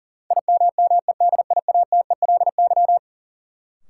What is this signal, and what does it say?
Morse code, heard as a keyed tone, about 40 wpm, 700 Hz. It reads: IMMEDIATELY
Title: Sentences from Top 1100 Words - New Word Drill Alt - 40wpm